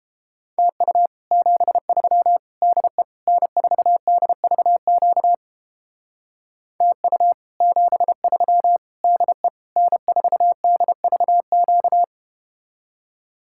Morse code, heard as a keyed tone, about 31 wpm, 700 Hz. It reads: TU 73 DE N4DVQ TU 73 DE N4DVQ